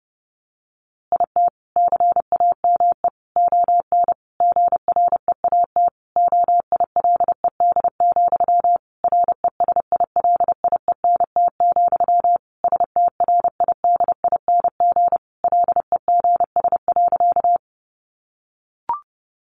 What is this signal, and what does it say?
Morse code, heard as a keyed tone, about 30 wpm, 700 Hz. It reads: IT CAME ON GREAT OILED, RESILIENT, STRIDING LEGS.
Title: A Sound Of Thunder 30wpm